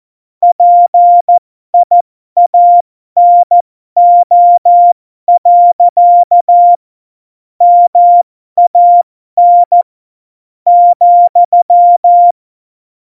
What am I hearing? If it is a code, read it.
Morse code, heard as a keyed tone, about 14 wpm, spaced as for 11 wpm, 700 Hz. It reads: PIANO. MAN,
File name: Morsekurs / Witze